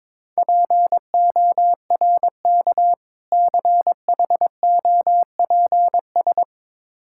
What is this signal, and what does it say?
Morse code, heard as a keyed tone, about 22 wpm, 690 Hz. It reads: PORK CHOPS
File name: Difficult Words with Hints - 22wpm